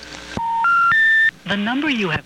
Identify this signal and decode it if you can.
Special information tone, with speech.